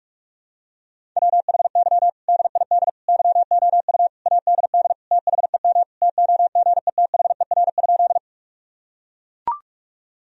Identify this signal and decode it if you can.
Morse code, heard as a keyed tone, about 41 wpm, 700 Hz. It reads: WHY DID YOU ADD THEM TOGETHER?
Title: Sentences from Top 1000 Words - Review Part 1 - 45wpm